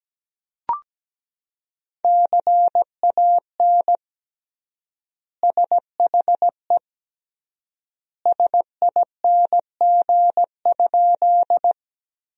Morse code, heard as a keyed tone, about 17 wpm, 700 Hz. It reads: CAN SHE SING?
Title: 3x Word-Spacing - Sentences from Top 400 Words - 17wpm